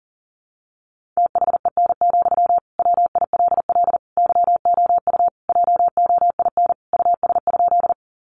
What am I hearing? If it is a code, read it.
Morse code, heard as a keyed tone, about 40 wpm, 700 Hz. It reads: THEN, WILL YOU JOIN US?
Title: Sentences from Top 1400 Words 40wpm